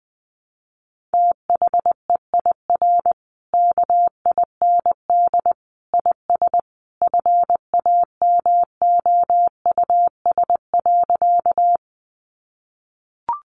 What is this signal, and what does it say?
Morse code, heard as a keyed tone, about 20 wpm, 700 Hz. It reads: THEIR KIND IS FAMOUS.
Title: Sentences from Top 1500 Words 20wpm